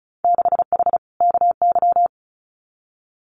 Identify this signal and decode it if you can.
Morse code, heard as a keyed tone, about 36 wpm, 700 Hz. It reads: 6H KY
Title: Winter Field Day Exchanges Encore 35wpm